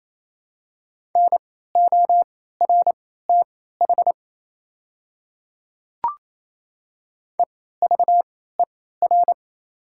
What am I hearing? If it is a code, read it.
Morse code, heard as a keyed tone, about 27 wpm, spaced as for 16 wpm, 700 Hz. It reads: NORTH EVER
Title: 3x Character Spacing - Top 500 Words - 28wpm